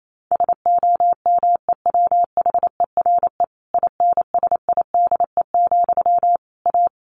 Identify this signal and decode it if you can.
Morse code, heard as a keyed tone, about 28 wpm, 700 Hz. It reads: SOMEWHERE INSIDE, A
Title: A Sound Of Thunder 28wpm